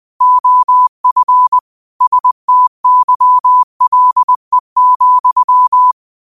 Morse code, heard as a keyed tone, about 20 wpm, 1000 Hz. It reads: OF STYLE,
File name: qotdfiles / 20wpm